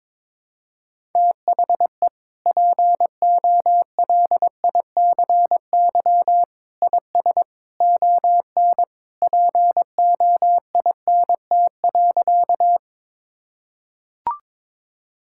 Morse code, heard as a keyed tone, about 22 wpm, 700 Hz. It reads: THE POLICY IS ON POINT.
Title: Sentences from Top 1000 Words - Review Part 5 - 22wpm